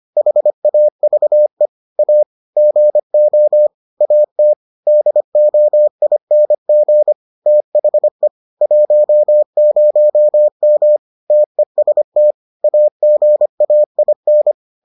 Morse code, heard as a keyed tone, about 25 wpm, 600 Hz. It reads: HAVE A GO AT DOING THE 10M TEST AGAIN